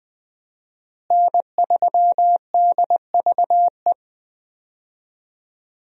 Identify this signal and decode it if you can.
Morse code, heard as a keyed tone, about 21 wpm, 700 Hz. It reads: N3DVE
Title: USA Call Signs - 5 Characters - Repeated 3x - 20wpm